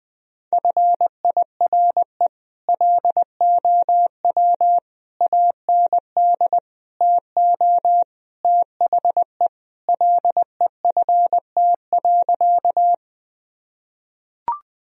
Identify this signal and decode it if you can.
Morse code, heard as a keyed tone, about 19 wpm, 700 Hz. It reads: FIRE LOW AND TO THE LEFT.
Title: Sentences from Top 1000 Words - Review Part 1 - 20wpm